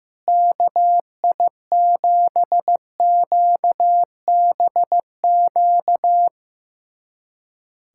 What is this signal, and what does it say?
Morse code, heard as a keyed tone, about 15 wpm, 700 Hz. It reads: KI7QBQ
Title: USA Callsign DE Callsign 15wpm